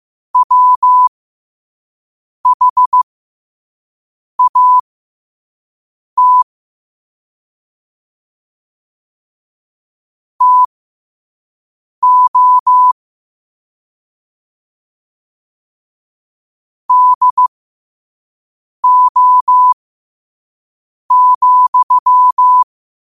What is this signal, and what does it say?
Morse code, heard as a keyed tone, about 15 wpm, spaced as for 5 wpm, 1000 Hz. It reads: WHAT TO DO,
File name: qotdfiles / 5wpm